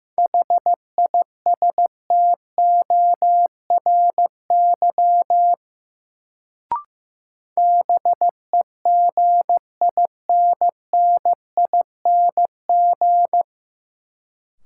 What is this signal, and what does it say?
Morse code, heard as a keyed tone, about 16 wpm, 700 Hz. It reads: HISTORY BEGINNING
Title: Sets of 3 Words - New Word Drill - 15wpm